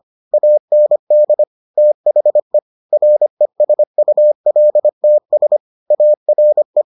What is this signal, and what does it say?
Morse code, heard as a keyed tone, about 25 wpm, 600 Hz. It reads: AND THE RESULTS ARE